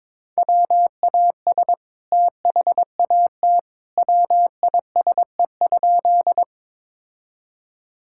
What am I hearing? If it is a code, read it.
Morse code, heard as a keyed tone, about 21 wpm, 700 Hz. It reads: WAS THAT WISE?
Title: Sentences from Top 1600 Words 22wpm